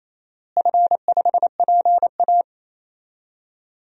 Morse code, heard as a keyed tone, about 28 wpm, 700 Hz. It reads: F5PA
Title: DXCC-100 Call Signs - 4 Characters - Repeated 2x - 28wpm